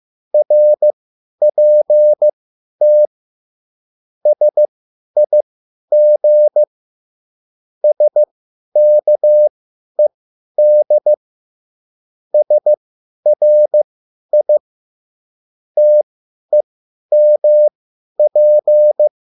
Morse code, heard as a keyed tone, about 15 wpm, spaced as for 10 wpm, 600 Hz. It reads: RPT SIG SKED SRI TEMP